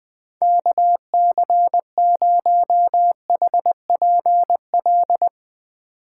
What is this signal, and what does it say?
Morse code, heard as a keyed tone, about 20 wpm, 700 Hz. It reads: KC0HPL